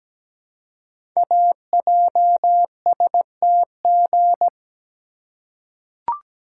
Morse code, heard as a keyed tone, about 17 wpm, 700 Hz. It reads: AJSTG
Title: Code Groups - 5 Letters - 17wpm